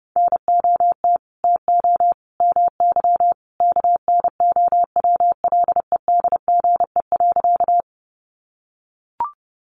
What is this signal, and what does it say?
Morse code, heard as a keyed tone, about 30 wpm, 700 Hz. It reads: NOT TO MY KNOWLEDGE.
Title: Sentences from Top 600 Words 30wpm